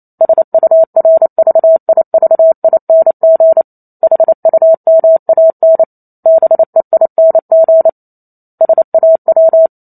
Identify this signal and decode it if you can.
Morse code, heard as a keyed tone, about 28 wpm, 650 Hz. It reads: SURVIVING HUMAN BEING SAW